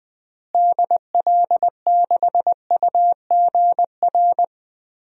Morse code, heard as a keyed tone, about 20 wpm, 700 Hz. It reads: DL6UGR
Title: DXCC-100 Call Signs - 6 Characters - Repeated 3x - 20wpm